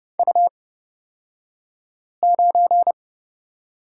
Morse code, heard as a keyed tone, about 29 wpm, spaced as for 5 wpm, 700 Hz. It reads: U9
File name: Lesson 37d - Letter x - Two Characters 30wpm